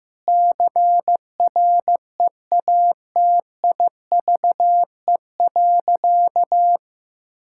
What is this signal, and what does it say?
Morse code, heard as a keyed tone, about 15 wpm, 700 Hz. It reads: CREATIVE.
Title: Sentences from Top 1000 Words - Review Part 5 - 15wpm